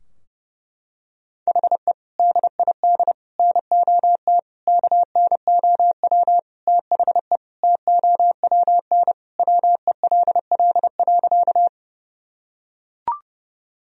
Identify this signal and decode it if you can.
Morse code, heard as a keyed tone, about 30 wpm, 700 Hz. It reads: HE DID NOT KNOW THE TOWN WELL.
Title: Sentences from Top 1000 Words - Review Part 2 - 30wpm